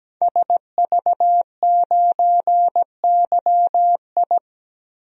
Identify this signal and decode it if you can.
Morse code, heard as a keyed tone, about 17 wpm, 700 Hz. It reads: SV9YI
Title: DXCC-100 Call Signs - 5 Characters - 17wpm